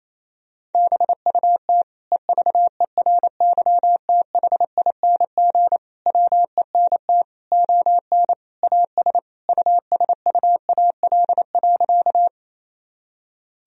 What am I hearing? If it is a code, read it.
Morse code, heard as a keyed tone, about 28 wpm, 700 Hz. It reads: BUT EVERYTHING WENT ON AS USUAL.
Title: Sentences from Top 1000 Words - Review Part 2 - 28wpm